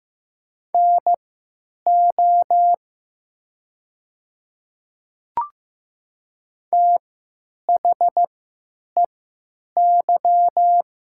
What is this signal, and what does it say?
Morse code, heard as a keyed tone, about 15 wpm, spaced as for 9 wpm, 700 Hz. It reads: NO THEY
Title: 3x Character Spacing - Top 200 Words - 15wpm